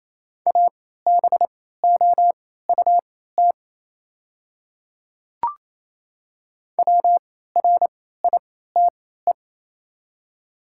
Morse code, heard as a keyed tone, about 28 wpm, spaced as for 16 wpm, 690 Hz. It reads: ABOUT WRITE
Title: 3x Character Spacing - Top 100 Words - 28wpm